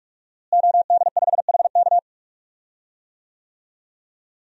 Morse code, heard as a keyed tone, about 46 wpm, 700 Hz. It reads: OD5HK
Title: DXCC-100 Call Signs - 5 Characters - Repeated 3x - 45wpm